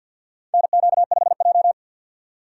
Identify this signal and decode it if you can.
Morse code, heard as a keyed tone, about 52 wpm, 700 Hz. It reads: NQ5J
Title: USA Call Signs - 4x Character Spacing - 50wpm